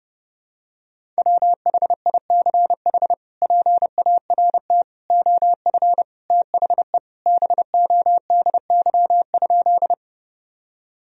Morse code, heard as a keyed tone, about 32 wpm, 700 Hz. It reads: WHICH PART OF THE BODY?
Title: Sentences from Top 1000 Words - Review Part 2 - 30wpm